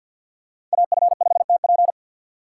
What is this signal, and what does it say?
Morse code, heard as a keyed tone, about 52 wpm, 700 Hz. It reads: AF5TP
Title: USA Call Signs - 5 Characters - 8x Character Spacing - 50wpm